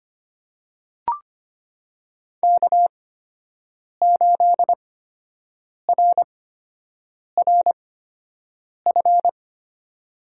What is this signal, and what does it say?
Morse code, heard as a keyed tone, about 25 wpm, spaced as for 7 wpm, 700 Hz. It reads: K8RRF
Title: USA Call Signs - 8x Character Spacing - 25wpm